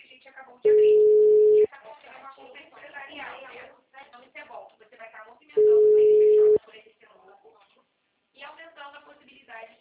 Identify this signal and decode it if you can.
Ringback tone (Europe, 425 Hz), with speech.
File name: out- -2008- - - .92501